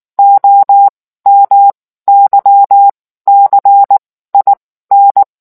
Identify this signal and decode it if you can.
Morse code, heard as a keyed tone, about 19 wpm, spaced as for 14 wpm, 800 Hz. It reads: OMYCIN